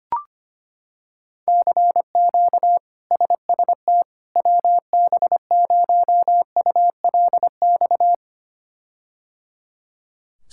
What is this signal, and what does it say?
Morse code, heard as a keyed tone, about 25 wpm, 700 Hz. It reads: CQ SST WB0ULX
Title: K1USN SST QSO Practice - 25wpm